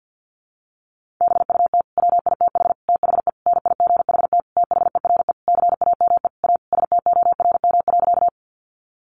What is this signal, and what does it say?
Morse code, heard as a keyed tone, about 50 wpm, 700 Hz. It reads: BUT WITH THE NIGHT THERE CAME A STORM.
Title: Sentences from Top 1500 Words 50wpm